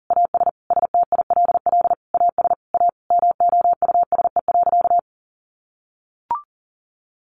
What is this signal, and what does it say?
Morse code, heard as a keyed tone, about 40 wpm, 700 Hz. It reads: AS STILL AS A MOUSE.
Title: Sentences from Top 700 Words 40wpm